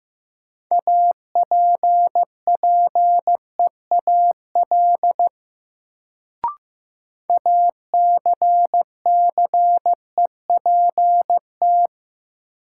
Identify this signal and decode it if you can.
Morse code, heard as a keyed tone, about 15 wpm, 700 Hz. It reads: APPEAL ACCEPT
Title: Sentences from Top 1300 Words - New Word Drill - 15wpm